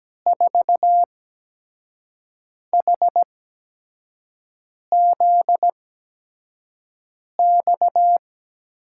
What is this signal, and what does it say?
Morse code, heard as a keyed tone, about 17 wpm, spaced as for 5 wpm, 700 Hz. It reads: 4HZX
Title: USA Call Signs - 6 Characters - 8x Character Spacing - 17wpm